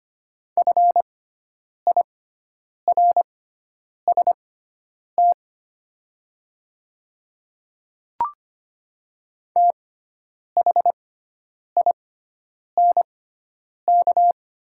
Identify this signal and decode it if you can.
Morse code, heard as a keyed tone, about 25 wpm, spaced as for 9 wpm, 700 Hz. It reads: FIRST THINK
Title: 6x Character Spacing - Top 300 Words - 25wpm